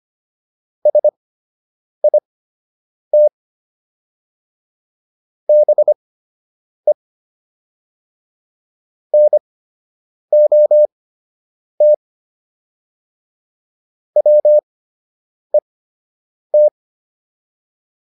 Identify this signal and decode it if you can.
Morse code, heard as a keyed tone, about 25 wpm, spaced as for 8 wpm, 600 Hz. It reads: SIT BE NOT WET